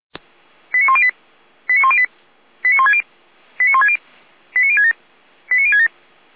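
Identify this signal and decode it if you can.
Five-tone selective call (ZVEI), heard as quick runs of five tones, tones of about 70 ms. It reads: 89108 89108 89160 89160 89067 89067
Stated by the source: ZVEI five-tone selective call received by SDR.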